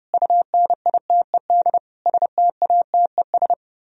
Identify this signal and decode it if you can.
Morse code, heard as a keyed tone, about 30 wpm, 700 Hz. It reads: UNITED STATES